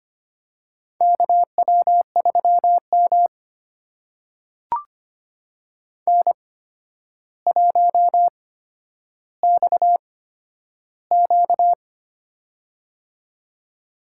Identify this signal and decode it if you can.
Morse code, heard as a keyed tone, about 25 wpm, spaced as for 11 wpm, 700 Hz. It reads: KW3M N1XQ
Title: USA Call Signs - 4 Characters - 8x Character Spacing - 25wpm